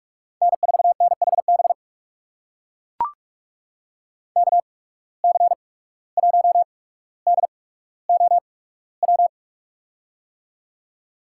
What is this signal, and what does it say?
Morse code, heard as a keyed tone, about 42 wpm, spaced as for 18 wpm, 700 Hz. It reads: N4NHB KC1DOW
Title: USA Call Signs - 8x Character Spacing - 45wpm